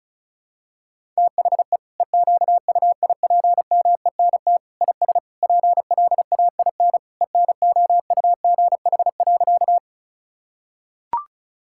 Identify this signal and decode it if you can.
Morse code, heard as a keyed tone, about 36 wpm, 700 Hz. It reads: THE EQUIPMENT IS PLAIN ENOUGH.
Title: Sentences from Top 1000 Words - Review Part 5 - 35wpm